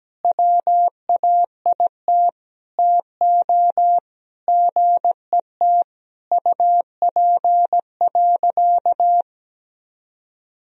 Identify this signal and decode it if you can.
Morse code, heard as a keyed tone, about 17 wpm, 700 Hz. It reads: WAIT TO GET UP.
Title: Sentences from Top 500 Words and Abbreviations 17wpm